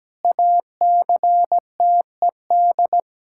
Morse code, heard as a keyed tone, about 17 wpm, 700 Hz. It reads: ACTED
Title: Pangram Challenge v2 - 17wpm